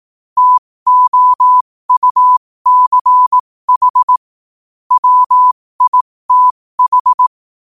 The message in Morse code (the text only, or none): TOUCH WITH